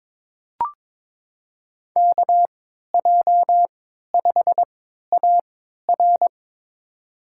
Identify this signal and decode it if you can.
Morse code, heard as a keyed tone, about 22 wpm, spaced as for 13 wpm, 700 Hz. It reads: KJ5AR
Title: USA Call Signs - 5 Characters - 3x Character Spacing - 22wpm